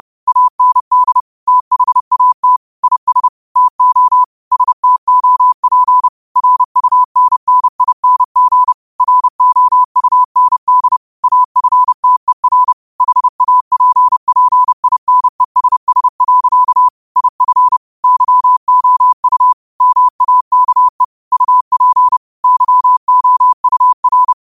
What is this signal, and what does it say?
Morse code, heard as a keyed tone, about 30 wpm, 1000 Hz. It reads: AND THAT IS TO STOP RUNNING ROUND AFTER HAPPINESS. IF YOU MAKE UP YOUR
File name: qotdfiles / 30wpm